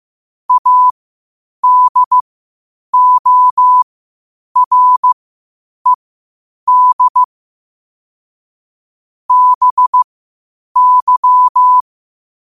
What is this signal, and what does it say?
Morse code, heard as a keyed tone, about 15 wpm, spaced as for 8 wpm, 1000 Hz. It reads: ADORED BY
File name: qotdfiles / 8wpm